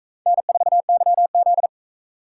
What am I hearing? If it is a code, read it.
Morse code, heard as a keyed tone, about 44 wpm, 700 Hz. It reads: N4YZ